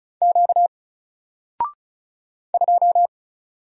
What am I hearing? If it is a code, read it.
Morse code, heard as a keyed tone, about 36 wpm, 700 Hz. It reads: Q 2
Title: Single Letter-Number 35wpm